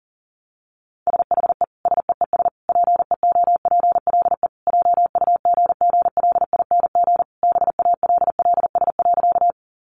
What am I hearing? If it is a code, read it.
Morse code, heard as a keyed tone, about 40 wpm, 700 Hz. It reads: SHE SEES PEOPLE JUGGLING BALLS.
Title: Easy Phrases 40wpm